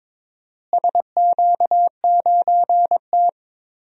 Morse code, heard as a keyed tone, about 22 wpm, 700 Hz. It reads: SQ9T
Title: DXCC-100 Call Signs - 4 Characters - Repeated 3x - 22wpm